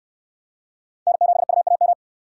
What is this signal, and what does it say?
Morse code, heard as a keyed tone, about 67 wpm, 700 Hz. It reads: N7UAM